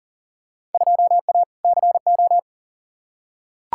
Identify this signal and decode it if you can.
Morse code, heard as a keyed tone, about 40 wpm, 700 Hz. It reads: 2A CO
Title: Field Day Exchanges - 40wpm